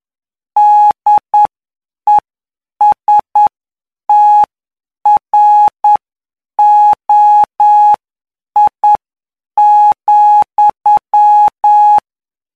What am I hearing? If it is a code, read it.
Morse code, heard as a keyed tone, about 9 wpm, spaced as for 7 wpm, 840 Hz. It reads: DESTROI,
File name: cw / py2lcd / 17-PY2LCD-Li��o 26